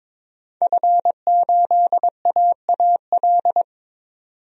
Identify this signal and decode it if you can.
Morse code, heard as a keyed tone, about 22 wpm, 700 Hz. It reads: F8AAL
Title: DXCC-100 Call Signs - 5 Characters - 22wpm